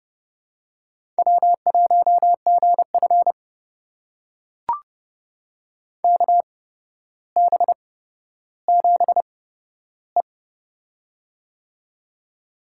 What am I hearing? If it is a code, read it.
Morse code, heard as a keyed tone, about 32 wpm, spaced as for 13 wpm, 700 Hz. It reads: W1GF KB7E
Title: USA Call Signs - 8x Character Spacing - 30wpm